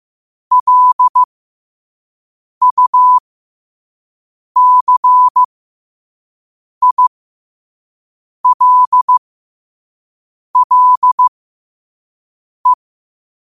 Morse code, heard as a keyed tone, about 15 wpm, spaced as for 5 wpm, 1000 Hz. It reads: LUCILLE
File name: qotdfiles / 5wpm